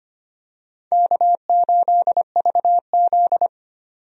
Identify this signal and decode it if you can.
Morse code, heard as a keyed tone, about 25 wpm, 700 Hz. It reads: K8VZ
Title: USA Call Signs - 4 Characters - 25wpm